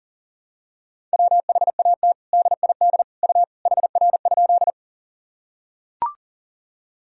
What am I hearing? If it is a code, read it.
Morse code, heard as a keyed tone, about 37 wpm, 700 Hz. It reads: WHAT DID U HR?
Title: Sentences from Top 300 Words and Abbreviations 40wpm